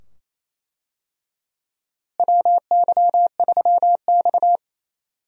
Morse code, heard as a keyed tone, about 27 wpm, 700 Hz. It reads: WY3X
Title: USA Call Signs - 4 Characters - Repeated 2x - 28wpm